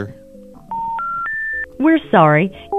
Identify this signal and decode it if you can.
Special information tone, with speech.